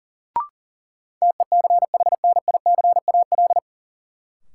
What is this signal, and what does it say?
Morse code, heard as a keyed tone, about 40 wpm, 700 Hz. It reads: TECHNICAL
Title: Sentences from Top 1100 Words - New Word Drill Alt - 40wpm